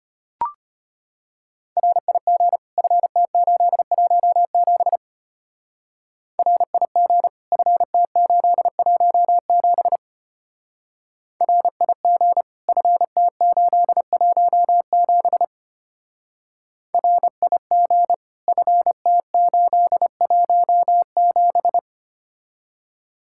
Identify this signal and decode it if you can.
Morse code, heard as a keyed tone, about 30 wpm, 700 Hz. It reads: RIG FT817 RIG FT817 RIG FT817 RIG FT817
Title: Speed-Racing Common QSO Phrases Encore - 25wpm